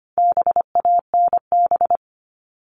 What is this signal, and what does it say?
Morse code, heard as a keyed tone, about 25 wpm, 700 Hz. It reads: BANB
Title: All Cut Numbers - 4-Digit 25wpm